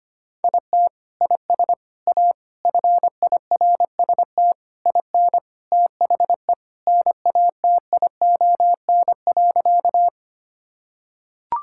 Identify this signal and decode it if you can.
Morse code, heard as a keyed tone, about 25 wpm, 700 Hz. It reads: IT IS A FIRST IN THE NATION.
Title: Sentences from Top 1000 Words - Review Part 6 - 25wpm